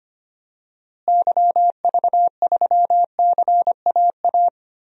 Morse code, heard as a keyed tone, about 25 wpm, 700 Hz. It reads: YV3CAA
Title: DXCC-100 Call Signs - 6 Characters - Repeated 2x - 25wpm